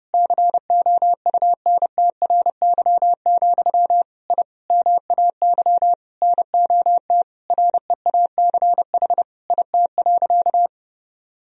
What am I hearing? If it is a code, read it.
Morse code, heard as a keyed tone, about 31 wpm, 700 Hz. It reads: COUNTRY, I MAY NOT REACH IT.
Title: Sentences from Top 1000 Words - Review Part 2 - 30wpm